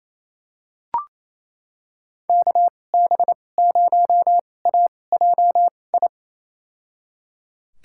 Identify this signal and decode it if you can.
Morse code, heard as a keyed tone, about 28 wpm, spaced as for 20 wpm, 700 Hz. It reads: KB0AJI
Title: USA Call Signs - 6 Characters - 2x Character Spacing - 28wpm